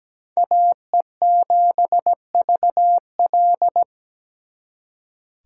Morse code, heard as a keyed tone, about 20 wpm, 700 Hz. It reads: AE7VL
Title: Speed Racing - USA 2x2 Call Signs 17wpm